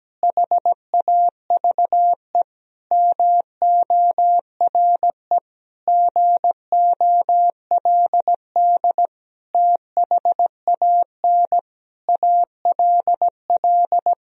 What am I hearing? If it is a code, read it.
Morse code, heard as a keyed tone, about 17 wpm, 700 Hz. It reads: HAVE MORE GOLD THAN ALL
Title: Sentences from Top 1000 Words - Review Part 3 - 17wpm